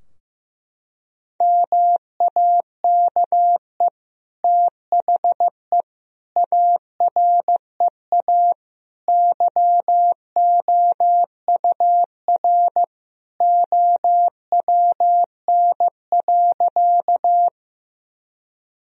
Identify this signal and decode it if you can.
Morse code, heard as a keyed tone, about 16 wpm, 700 Hz. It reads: MAKE THE AREA YOUR OWN.
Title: Sentences from Top 1000 Words - Review Part 2 - 15wpm